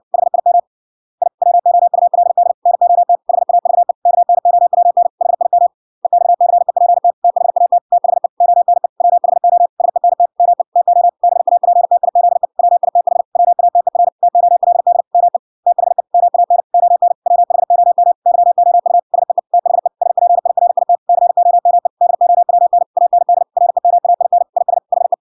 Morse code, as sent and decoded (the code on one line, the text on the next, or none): .... .. -- / .. / .-- --- ..- .-.. -.. / -. --- - / .... .- ...- . / -.- -. --- .-- -. / .... .. -- / . -..- -.-. .--. - / - .... .- - / - .... / --- -. / .-- .... --- / ... -. - / -- . / - --- / -... .- .--. - .. --.. / .-- .. - .... / .-- .- - . .-. / - --- .-.. -.. / -- . / - .... / -- .- -. / --- -. / .-- .... --- -- / -.-- --- ..- / ... . / - .... / ... .--. .. .-. .. - / -.-. --- -- . / -.. --- .-- -. / .- -. -.. / .-. -- .- .. -. / .. ... / .... .
HIM I WOULD NOT HAVE KNOWN HIM EXCEPT THAT THE ONE WHO SENT ME TO BAPTIZE WITH WATER TOLD ME THE MAN ON WHOM YOU SEE THE SPIRIT COME DOWN AND REMAIN IS HE